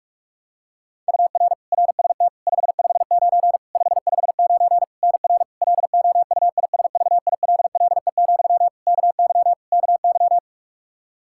Morse code, heard as a keyed tone, about 46 wpm, 700 Hz. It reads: UR RST 559 559 NR LOUISVILLE, KY KY
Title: Practice QSO - Exchange 1 - Beginning - 45wpm